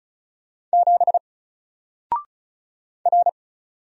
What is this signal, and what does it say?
Morse code, heard as a keyed tone, about 36 wpm, 700 Hz. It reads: 7 R
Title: Single Letter-Number Alt 35wpm